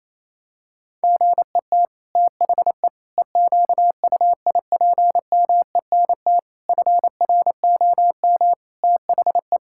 Morse code, heard as a keyed tone, about 28 wpm, 700 Hz. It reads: GET THE EQUIPMENT FROM THE
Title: Sentences from Top 1000 Words - Review Part 5 - 28wpm